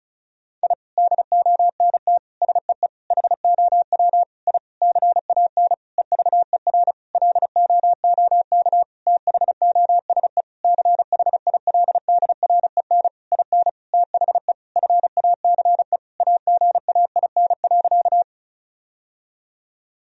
Morse code, heard as a keyed tone, about 34 wpm, 700 Hz. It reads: I DONT SEE HOW I CAN EVER LOOK THOSE CHILDREN IN THE FACE AGAIN.
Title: Sentences from Top 1000 Words - Review Part 2 - 35wpm